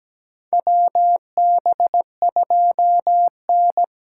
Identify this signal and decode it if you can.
Morse code, heard as a keyed tone, about 17 wpm, 700 Hz. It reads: WB2N
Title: USA Call Signs - 6 Characters - 8x Character Spacing - 17wpm